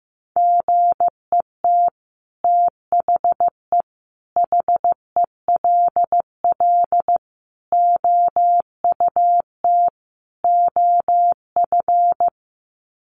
Morse code, heard as a keyed tone, about 15 wpm, 700 Hz. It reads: GET THE HELL OUT OF
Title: Sets of 5 Words Encore2 15wpm